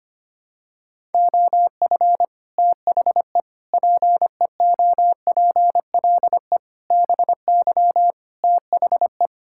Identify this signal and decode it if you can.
Morse code, heard as a keyed tone, about 25 wpm, 700 Hz. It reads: OF THE PEOPLE BY THE
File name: Sets of 9 Words - 25wpm